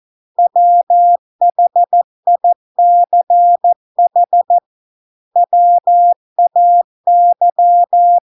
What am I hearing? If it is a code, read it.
Morse code, heard as a keyed tone, about 14 wpm, 700 Hz. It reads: WHICH WAY